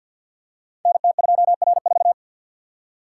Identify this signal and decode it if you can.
Morse code, heard as a keyed tone, about 52 wpm, 700 Hz. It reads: NT2R4
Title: Code Groups - 5 Letter:Numbers - 50wpm